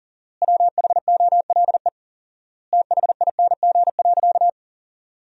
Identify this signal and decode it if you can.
Morse code, heard as a keyed tone, about 40 wpm, 700 Hz. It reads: WHOLE THING.
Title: 4x Word-Spacing - Sentences from Top 400 Words - 40wpm